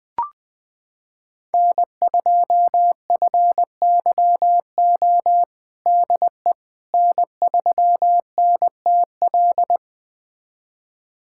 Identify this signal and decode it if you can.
Morse code, heard as a keyed tone, about 20 wpm, 700 Hz. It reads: N2FYO DE N3NTL
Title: USA Callsign DE Callsign - 5 Characters - 20wpm